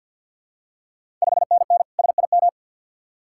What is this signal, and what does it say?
Morse code, heard as a keyed tone, about 50 wpm, 700 Hz. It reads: 5NN SIM
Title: Ontario QSO Party Exchange 50wpm